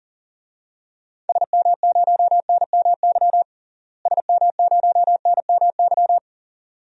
Morse code, heard as a keyed tone, about 37 wpm, 700 Hz. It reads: SM0NMY SM0NMY
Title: DXCC-100 Call Signs - 6 Characters - Repeated 2x - 40wpm